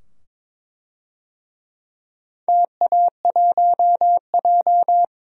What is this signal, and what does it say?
Morse code, heard as a keyed tone, about 22 wpm, 700 Hz. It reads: TA1J
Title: DXCC-100 Call Signs - 4 Characters - 22wpm